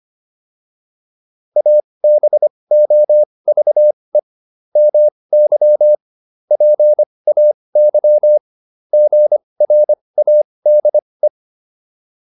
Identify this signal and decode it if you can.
Morse code, heard as a keyed tone, about 25 wpm, spaced as for 20 wpm, 600 Hz. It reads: ABOVE MY PAY GRADE